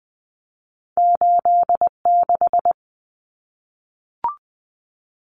Morse code, heard as a keyed tone, about 20 wpm, 700 Hz. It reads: 86
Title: Common Cut Numbers - 2-Digit 20wpm